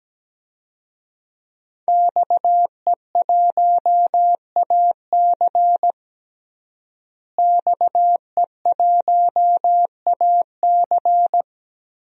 Morse code, heard as a keyed tone, about 17 wpm, 700 Hz. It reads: XE1AC XE1AC
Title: DXCC-100 Call Signs - 5 Characters - Repeated 3x - 17wpm